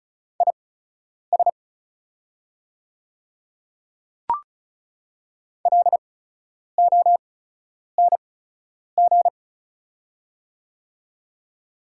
Morse code, heard as a keyed tone, about 36 wpm, spaced as for 10 wpm, 700 Hz. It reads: IS LONG
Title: 8x Character Spacing - Top 100 Words - 35wpm